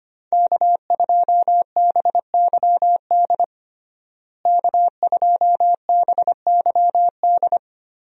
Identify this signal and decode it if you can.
Morse code, heard as a keyed tone, about 25 wpm, 700 Hz. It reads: K2BYD K2BYD
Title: USA Call Signs - 5 Characters - Repeated 2x - 25wpm